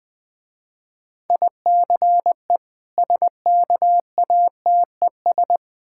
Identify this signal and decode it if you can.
Morse code, heard as a keyed tone, about 20 wpm, 700 Hz. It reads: ICE SKATES
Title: Difficult Words with Hints - 20wpm